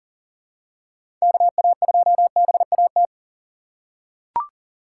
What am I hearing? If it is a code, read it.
Morse code, heard as a keyed tone, about 40 wpm, 700 Hz. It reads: KA2BAT
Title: USA Call Signs - 6 Characters - 4x Character Spacing - 40wpm